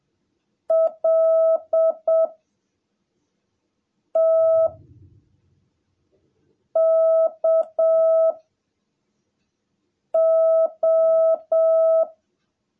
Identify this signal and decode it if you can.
Morse code, heard as a keyed tone, about 7 wpm, 660 Hz. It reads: L T K O
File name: jh8cbh / 2021cw / cw17a